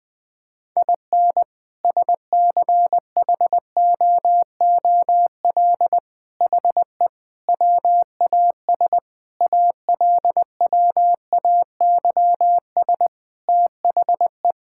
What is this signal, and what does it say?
Morse code, heard as a keyed tone, about 20 wpm, 700 Hz. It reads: IN SCHOOL HE WAS ALWAYS THE
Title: Sentences from Top 1000 Words - Review Part 5 - 20wpm